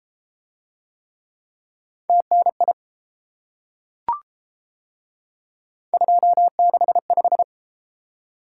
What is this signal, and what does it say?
Morse code, heard as a keyed tone, about 31 wpm, 700 Hz. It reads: TNI 265